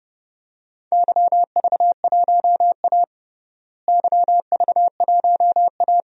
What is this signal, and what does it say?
Morse code, heard as a keyed tone, about 30 wpm, 700 Hz. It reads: YV1A YV1A
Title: DXCC-100 Call Signs - 4 Characters - Repeated 2x - 30wpm